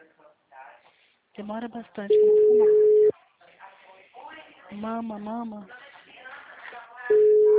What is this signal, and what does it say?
Ringback tone (Europe, 425 Hz), with speech.